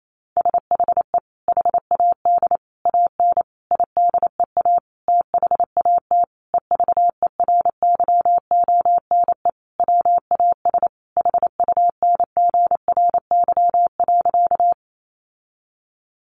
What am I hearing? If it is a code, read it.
Morse code, heard as a keyed tone, about 28 wpm, 700 Hz. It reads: SHE HAD AN IDEA THAT EVERYONE WAS HUNGRY.
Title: Sentences from Top 1400 Words 28wpm